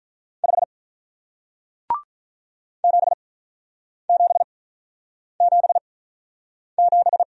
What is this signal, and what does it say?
Morse code, heard as a keyed tone, about 47 wpm, spaced as for 9 wpm, 700 Hz. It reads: 5 7777